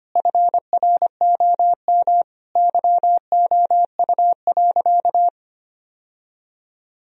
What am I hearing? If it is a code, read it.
Morse code, heard as a keyed tone, about 25 wpm, 700 Hz. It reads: FROM YOU.
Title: Sentences from Top 1000 Words - Review Part 1 - 25wpm